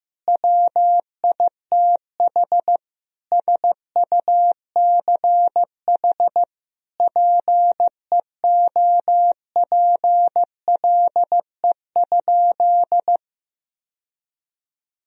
Morse code, heard as a keyed tone, about 16 wpm, 700 Hz. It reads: WITH SUCH PEOPLE?
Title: Sentences from Top 1000 Words - Review Part 3 - 15wpm